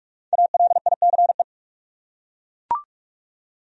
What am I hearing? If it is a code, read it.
Morse code, heard as a keyed tone, about 41 wpm, 700 Hz. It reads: ALICE
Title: USA Common Names - 45wpm